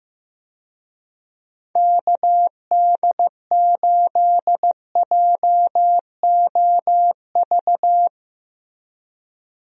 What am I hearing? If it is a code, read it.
Morse code, heard as a keyed tone, about 15 wpm, 700 Hz. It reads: KD8JOV
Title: USA Call Signs - 6 Characters - Repeated 3x - 15wpm